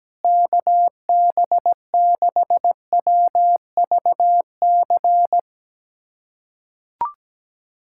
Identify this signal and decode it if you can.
Morse code, heard as a keyed tone, about 17 wpm, 700 Hz. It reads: KB6WVC
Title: USA Callsign DE Callsign - 6 Characters - 17wpm